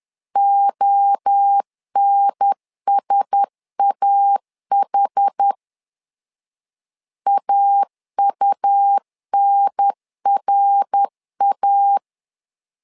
Morse code, heard as a keyed tone, about 10 wpm, 800 Hz. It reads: ONSAH AUNRA
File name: LEZIONI / 4 Q1S 10wpm 2133